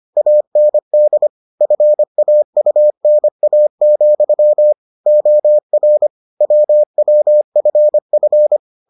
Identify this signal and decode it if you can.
Morse code, heard as a keyed tone, about 25 wpm, 600 Hz. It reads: AND FAUNA, OR WWFF